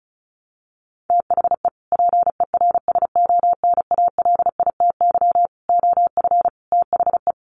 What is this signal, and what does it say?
Morse code, heard as a keyed tone, about 34 wpm, 700 Hz. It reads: THE PERSONALITY OF THE
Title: Sentences from Top 1400 Words 35wpm